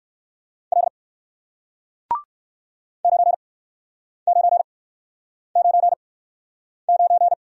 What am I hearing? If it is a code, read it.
Morse code, heard as a keyed tone, about 59 wpm, spaced as for 9 wpm, 700 Hz. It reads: L 9999